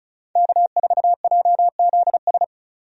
Morse code, heard as a keyed tone, about 34 wpm, 700 Hz. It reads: K4JZS
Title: USA Call Signs - 5 Characters - 35wpm